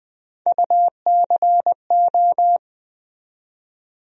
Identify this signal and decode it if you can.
Morse code, heard as a keyed tone, about 21 wpm, 700 Hz. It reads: UCO
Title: USA Call Sign Postfixes - 3 Letters 20wpm